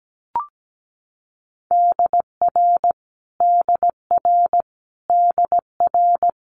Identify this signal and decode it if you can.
Morse code, heard as a keyed tone, about 17 wpm, 700 Hz. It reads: DR DR DR
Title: Multi-letter Phonemes Repeated 3x 17wpm